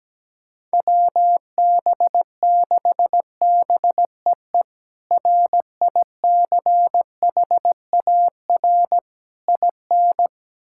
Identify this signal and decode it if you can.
Morse code, heard as a keyed tone, about 17 wpm, 700 Hz. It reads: WB6BEE RICHAR IN
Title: K1USN SST QSO Practice - 17wpm